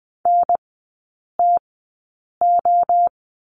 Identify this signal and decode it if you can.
Morse code, heard as a keyed tone, about 21 wpm, spaced as for 8 wpm, 700 Hz. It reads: NTO
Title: 047 Lesson 9e - Number 4 - Three Characters 20wpm